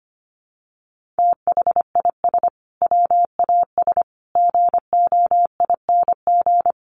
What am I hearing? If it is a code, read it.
Morse code, heard as a keyed tone, about 25 wpm, 700 Hz. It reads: THIS WAS GOING
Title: Sets of 5 Words Encore2 25wpm